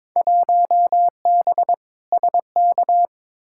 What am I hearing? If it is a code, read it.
Morse code, heard as a keyed tone, about 22 wpm, 700 Hz. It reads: 1B SK
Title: Field Day Exchanges Encore - 22wpm